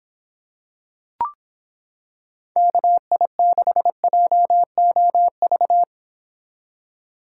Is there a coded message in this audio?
Yes: Morse code from 2.6 to 5.8 s, heard as a keyed tone, about 26 wpm, 700 Hz. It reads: KI6JOV